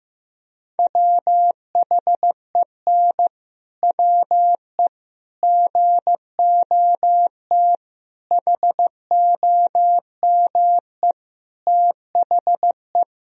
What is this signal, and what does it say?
Morse code, heard as a keyed tone, about 16 wpm, 700 Hz. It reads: WHEN WE GOT HOME THE
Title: Sentences from Top 500 Words and Abbreviations 15wpm